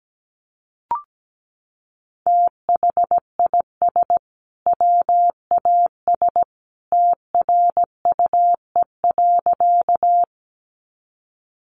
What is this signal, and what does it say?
Morse code, heard as a keyed tone, about 17 wpm, 700 Hz. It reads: THIS WAS TRUE.
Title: Sentences from Top 400 Words 17wpm